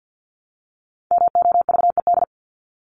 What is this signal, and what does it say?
Morse code, heard as a keyed tone, about 50 wpm, 700 Hz. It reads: MOVED
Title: ED Suffix - Top 100 Words with Base Words 50wpm